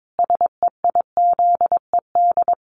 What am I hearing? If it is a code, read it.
Morse code, heard as a keyed tone, about 22 wpm, 700 Hz. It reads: SEIZED
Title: A Sound Of Thunder 22wpm